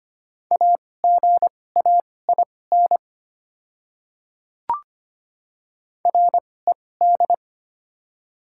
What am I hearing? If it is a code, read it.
Morse code, heard as a keyed tone, about 25 wpm, spaced as for 18 wpm, 700 Hz. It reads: AGAIN RED
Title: 2x Character Spacing - Top 500 Words - 25wpm